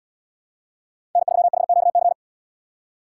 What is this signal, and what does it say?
Morse code, heard as a keyed tone, about 70 wpm, 700 Hz. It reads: N3HQC